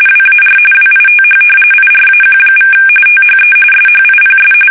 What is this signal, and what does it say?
RTTY, about 45.45 baud. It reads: WAR WITH IRAQ= THE SHIPS
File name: fsk850 / usnavy5